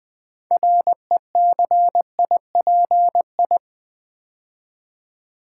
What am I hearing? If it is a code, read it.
Morse code, heard as a keyed tone, about 20 wpm, 700 Hz. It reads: RECIPI
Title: Build A Word - 3 Letters At A Time - 9-Letter Words - 20wpm